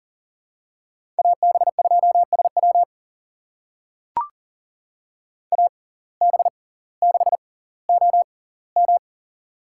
Morse code, heard as a keyed tone, about 38 wpm, spaced as for 20 wpm, 700 Hz. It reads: AB2SW AB6OM
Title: USA Call Signs - 5 Characters - 6x Character Spacing - 40wpm